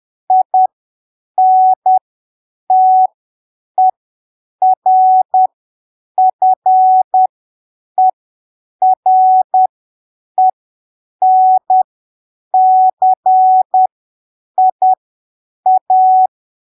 Morse code, heard as a keyed tone, about 10 wpm, spaced as for 7 wpm, 750 Hz. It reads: INTERFERENCIA
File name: arquivos / texto / texto 28 10ppm